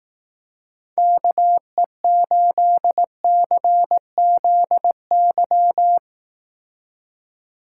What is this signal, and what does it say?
Morse code, heard as a keyed tone, about 18 wpm, 700 Hz. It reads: KE8CZY
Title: Speed Racing - USA 2x3 Call Signs 15wpm